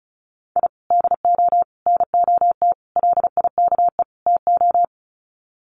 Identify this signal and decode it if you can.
Morse code, heard as a keyed tone, about 34 wpm, 700 Hz. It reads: I DO NOT LIKE TO
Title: Sets of 5 Words 35wpm